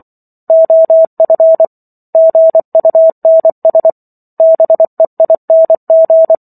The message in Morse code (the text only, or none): OF GUNS BEING